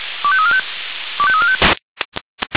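Five-tone selective call (ZVEI), heard as quick runs of five tones, tones of about 70 ms. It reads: 25345 25345
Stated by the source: ZVEI five-tone selective call received on an SDR.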